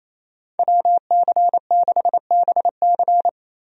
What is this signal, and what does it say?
Morse code, heard as a keyed tone, about 28 wpm, 700 Hz. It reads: WC6BC
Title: USA Call Signs - 5 Characters - Repeated 3x - 28wpm